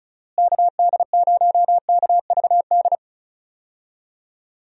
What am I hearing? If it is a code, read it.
Morse code, heard as a keyed tone, about 34 wpm, 700 Hz. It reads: KD0KVD
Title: USA Call Signs - 6 Characters - Repeated 2x - 35wpm